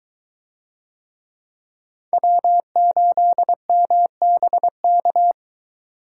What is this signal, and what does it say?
Morse code, heard as a keyed tone, about 23 wpm, 700 Hz. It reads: W8MBK